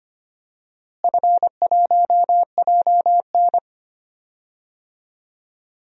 Morse code, heard as a keyed tone, about 25 wpm, 700 Hz. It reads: F1JN
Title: DXCC-100 Call Signs - 4 Characters - Repeated 2x - 25wpm